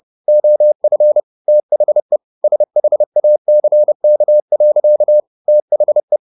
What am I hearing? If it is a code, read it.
Morse code, heard as a keyed tone, about 30 wpm, 600 Hz. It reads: OF THE SHACK. THE
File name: the old man and the sea / the old man and the sea 1 30wpm